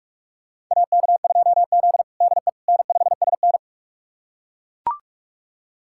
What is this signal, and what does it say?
Morse code, heard as a keyed tone, about 41 wpm, 700 Hz. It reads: AK2Z DE N5SN
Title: USA Callsign DE Callsign - 4 Characters - 45wpm